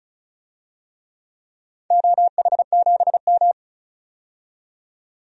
Morse code, heard as a keyed tone, about 34 wpm, 700 Hz. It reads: OH7M
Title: DXCC-100 Call Signs - 4 Characters - 35wpm